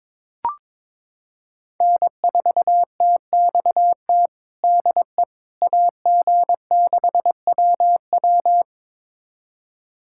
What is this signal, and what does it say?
Morse code, heard as a keyed tone, about 22 wpm, 700 Hz. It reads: N4TXT DE AG6WW
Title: USA Callsign DE Callsign - 5 Characters - 22wpm